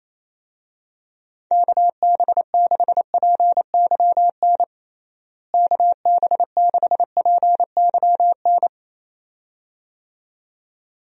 Morse code, heard as a keyed tone, about 28 wpm, 700 Hz. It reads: KB6PYN KB6PYN
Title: USA Call Signs - 6 Characters - Repeated 2x - 28wpm